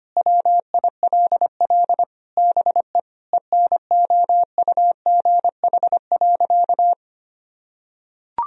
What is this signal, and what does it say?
Morse code, heard as a keyed tone, about 25 wpm, 700 Hz. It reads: WILL BE ENOUGH.
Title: Sentences from Top 1000 Words - Review Part 2 - 25wpm